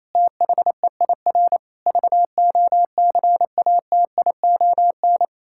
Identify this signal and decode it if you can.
Morse code, heard as a keyed tone, about 27 wpm, 700 Hz. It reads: THEIR VOCATION
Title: Pangram Challenge v2 - 28wpm